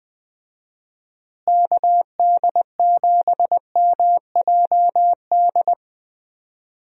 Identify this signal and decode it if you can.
Morse code, heard as a keyed tone, about 20 wpm, 700 Hz. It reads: KD7MJD
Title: USA Call Signs - 6 Characters - Repeated 3x - 20wpm